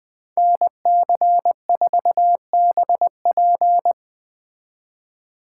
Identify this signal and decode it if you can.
Morse code, heard as a keyed tone, about 20 wpm, 700 Hz. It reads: NC4BP
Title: Speed Racing - USA 2x2 Call Signs 17wpm